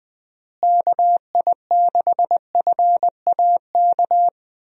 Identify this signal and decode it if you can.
Morse code, heard as a keyed tone, about 20 wpm, 700 Hz. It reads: KI6FAK
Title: USA Call Signs - 6 Characters - Repeated 3x - 20wpm